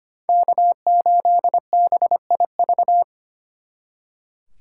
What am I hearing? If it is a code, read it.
Morse code, heard as a keyed tone, about 33 wpm, 700 Hz. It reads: K8BIV